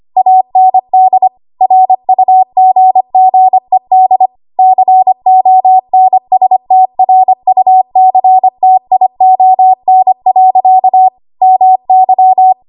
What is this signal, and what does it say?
Morse code, heard as a keyed tone, about 25 wpm, 750 Hz. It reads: AND RUGGED CONSTRUCTION. MY
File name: Archive / 25 WPM / 25WPM